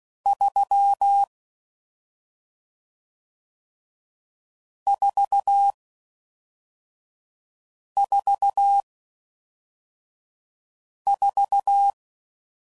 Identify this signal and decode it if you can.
Morse code, heard as a keyed tone, about 16 wpm, 800 Hz. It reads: 3 4 4 4